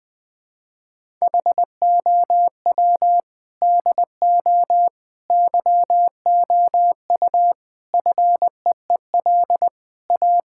Morse code, heard as a keyed tone, about 20 wpm, 700 Hz. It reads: HOW DO YOU FEEL A
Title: Build A Phrase - 5-Word Phrases - 20wpm